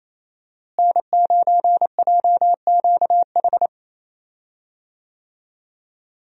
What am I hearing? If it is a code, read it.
Morse code, heard as a keyed tone, about 27 wpm, 700 Hz. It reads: N9JQH
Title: USA Call Signs - 5 Characters - Repeated 2x - 28wpm